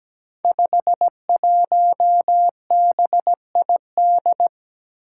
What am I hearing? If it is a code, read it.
Morse code, heard as a keyed tone, about 17 wpm, 700 Hz. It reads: 51BID
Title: Five Challenging Letter-Number Combinations 17wpm